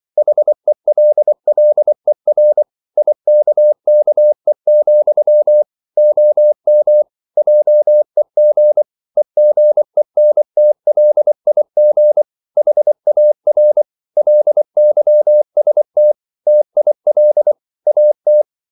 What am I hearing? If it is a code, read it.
Morse code, heard as a keyed tone, about 24 wpm, 600 Hz. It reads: HELLER IKKE, OM JEG EGENTLIG HAR LYST TIL AT
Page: Ferie 24 wpm